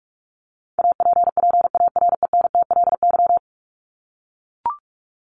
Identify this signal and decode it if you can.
Morse code, heard as a keyed tone, about 44 wpm, 700 Hz. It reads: APPARENTLY
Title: LY Suffix - Top 100 Words with Base Words 45wpm